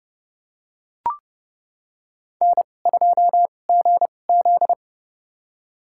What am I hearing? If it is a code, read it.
Morse code, heard as a keyed tone, about 29 wpm, spaced as for 22 wpm, 700 Hz. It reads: N2GZ
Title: USA Call Signs - 2x Character Spacing - 30wpm